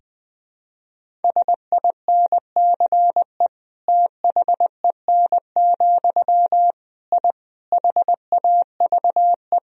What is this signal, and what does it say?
Morse code, heard as a keyed tone, about 20 wpm, 700 Hz. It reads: SINCE THEN, I HAVE
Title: Sentences from Top 1000 Words - Review Part 1 - 20wpm